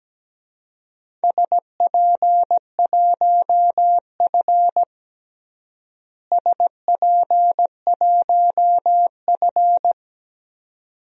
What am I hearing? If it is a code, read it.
Morse code, heard as a keyed tone, about 17 wpm, 700 Hz. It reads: SP1F SP1F
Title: DXCC-100 Call Signs - 4 Characters - Repeated 2x - 17wpm